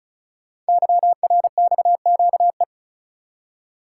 Morse code, heard as a keyed tone, about 36 wpm, 700 Hz. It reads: YRXQE
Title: Code Groups - 5 Letters - 35wpm